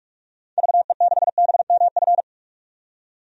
Morse code, heard as a keyed tone, about 46 wpm, 700 Hz. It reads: VE6BMF
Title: DXCC-100 Call Signs - 6 Characters - 45wpm